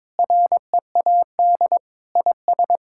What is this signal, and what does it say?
Morse code, heard as a keyed tone, about 22 wpm, 700 Hz. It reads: READ IS
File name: practice / 22wpm / Sentences with 3-Word Phrases - 22wpm